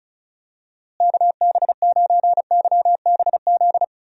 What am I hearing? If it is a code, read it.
Morse code, heard as a keyed tone, about 36 wpm, 700 Hz. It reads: KB9YBZ
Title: USA Call Signs - 6 Characters - Repeated 3x - 35wpm